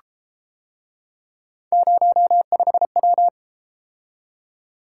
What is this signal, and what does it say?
Morse code, heard as a keyed tone, about 32 wpm, 700 Hz. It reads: 05W